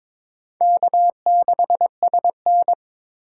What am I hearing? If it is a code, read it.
Morse code, heard as a keyed tone, about 21 wpm, 700 Hz. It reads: K6SN
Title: USA Call Signs - 4 Characters - 2x Character Spacing - 22wpm